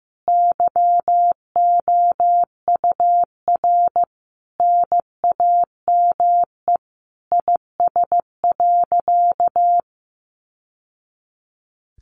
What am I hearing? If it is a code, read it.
Morse code, heard as a keyed tone, about 15 wpm, 700 Hz. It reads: YOUR NAME IS.
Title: Sentences from Top 1100 Words 15wpm